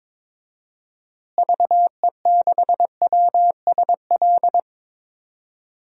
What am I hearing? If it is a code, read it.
Morse code, heard as a keyed tone, about 22 wpm, 700 Hz. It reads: VE6WSL
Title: DXCC-100 Call Signs - 6 Characters - Repeated 2x - 22wpm